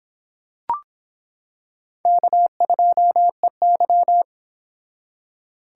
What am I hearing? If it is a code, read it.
Morse code, heard as a keyed tone, about 26 wpm, 700 Hz. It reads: K2EY